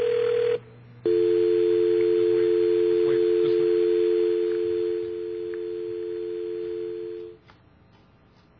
Dial tone (North America), 350 + 440 Hz.